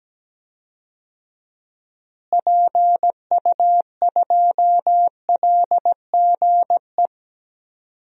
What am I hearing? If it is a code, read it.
Morse code, heard as a keyed tone, about 17 wpm, 700 Hz. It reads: PU2LGE
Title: DXCC-100 Call Signs - 6 Characters - Repeated 2x - 17wpm